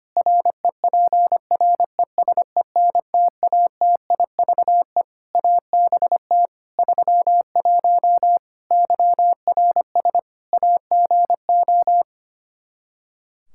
Morse code, heard as a keyed tone, about 25 wpm, 700 Hz. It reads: REPRESENTATIVE ABT 31 YRS AGO
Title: Practice QSO - Exchange 3 - More Info - 25wpm